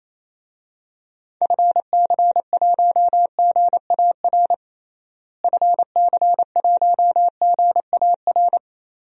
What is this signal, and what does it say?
Morse code, heard as a keyed tone, about 28 wpm, 700 Hz. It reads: FC1GAR FC1GAR
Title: DXCC-100 Call Signs - 6 Characters - Repeated 3x - 28wpm